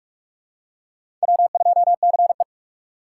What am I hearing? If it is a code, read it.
Morse code, heard as a keyed tone, about 48 wpm, 700 Hz. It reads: W2CE
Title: USA Call Signs - 4 Characters - 8x Character Spacing - 45wpm